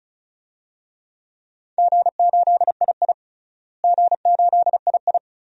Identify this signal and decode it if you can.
Morse code, heard as a keyed tone, about 36 wpm, 700 Hz. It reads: G8II G8II
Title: DXCC-100 Call Signs - 4 Characters - Repeated 2x - 35wpm